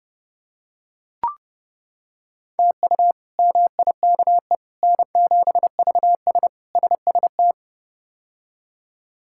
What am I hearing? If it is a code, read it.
Morse code, heard as a keyed tone, about 30 wpm, 700 Hz. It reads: TU MIKE N7VS SST
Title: K1USN SST QSO Practice - 30wpm